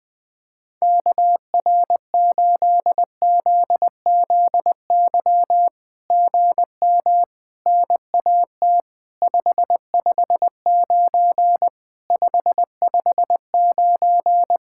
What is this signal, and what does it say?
Morse code, heard as a keyed tone, about 19 wpm, 700 Hz. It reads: KR8ZZY GM NAT 559 559
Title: SOTA QSO Practice 20wpm